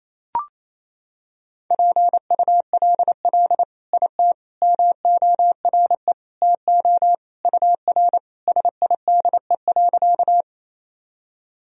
Morse code, heard as a keyed tone, about 28 wpm, 700 Hz. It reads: PULL IT MORE TO UR SIDE.
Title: Sentences from Top 400 Words and Abbreviations 28wpm